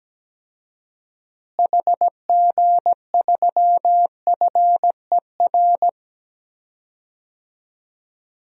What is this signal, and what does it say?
Morse code, heard as a keyed tone, about 17 wpm, 700 Hz. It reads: HG3FER
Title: DXCC-100 Call Signs - 6 Characters - Repeated 2x - 17wpm